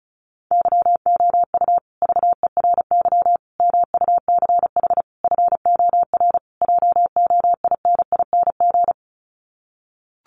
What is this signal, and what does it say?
Morse code, heard as a keyed tone, about 34 wpm, 700 Hz. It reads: YOU VERY MUCH FOR JOINING
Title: Sets of 5 Words Encore 35wpm